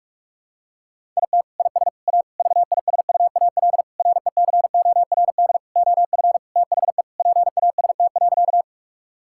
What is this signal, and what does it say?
Morse code, heard as a keyed tone, about 46 wpm, 700 Hz. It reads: IT IS A VISUAL RECORD OF THE PAST.
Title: Sentences from Top 1000 Words - Review Part 6 - 45wpm